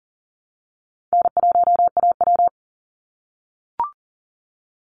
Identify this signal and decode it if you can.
Morse code, heard as a keyed tone, about 40 wpm, 700 Hz. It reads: N1AW
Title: NA SOTA Chaser Callsigns 40wpm